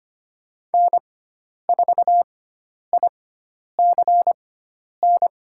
Morse code, heard as a keyed tone, about 25 wpm, spaced as for 10 wpm, 700 Hz. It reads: N4ICN
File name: Lesson 16g - Letter c - Call Signs 25wpm